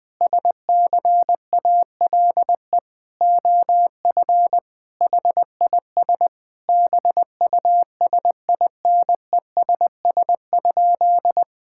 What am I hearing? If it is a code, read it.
Morse code, heard as a keyed tone, about 20 wpm, 700 Hz. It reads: SCALE OF HIS BUSINESS?
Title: Sentences from Top 1000 Words - Review Part 5 - 20wpm